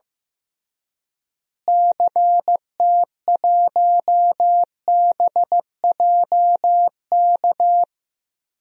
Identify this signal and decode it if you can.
Morse code, heard as a keyed tone, about 15 wpm, 700 Hz. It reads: CT1BJK
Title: DXCC-100 Call Signs - 6 Characters - Repeated 2x - 15wpm